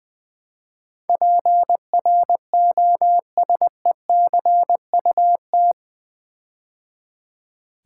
Morse code, heard as a keyed tone, about 20 wpm, 700 Hz. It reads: PROSECUT
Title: Build A Word - 4 Letters At A Time - 10-Letter Words - 20wpm